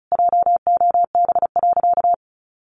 Morse code, heard as a keyed tone, about 34 wpm, 700 Hz. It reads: JOB.
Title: Sentences from Top 1500 Words 35wpm